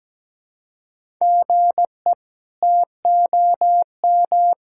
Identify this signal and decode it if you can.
Morse code, heard as a keyed tone, about 17 wpm, 700 Hz. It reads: GE TOM
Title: K1USN SST QSO Practice - 17wpm